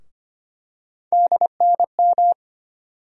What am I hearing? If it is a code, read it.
Morse code, heard as a keyed tone, about 25 wpm, 700 Hz. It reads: DNM
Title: USA Call Sign Postfixes - 3 Letters 25wpm